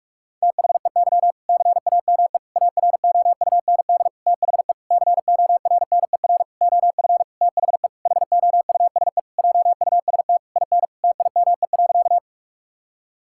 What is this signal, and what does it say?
Morse code, heard as a keyed tone, about 46 wpm, 700 Hz. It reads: THEY CAME AROUND THE CORNER OF THE HOUSE JUST IN TIME.
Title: Sentences from Top 1600 Words 45wpm